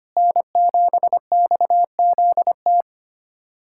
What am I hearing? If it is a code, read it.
Morse code, heard as a keyed tone, about 25 wpm, 700 Hz. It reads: N7XZT
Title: USA Callsign DE Callsign - 5 Characters - 25wpm